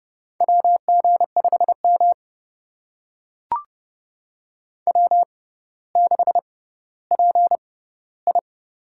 Morse code, heard as a keyed tone, about 30 wpm, spaced as for 15 wpm, 700 Hz. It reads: WG5M W6PI
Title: USA Call Signs - 4 Characters - 6x Character Spacing - 30wpm